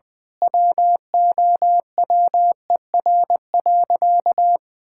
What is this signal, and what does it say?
Morse code, heard as a keyed tone, about 20 wpm, 700 Hz. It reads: WOWER.
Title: Pangram Challenge v2 - 20wpm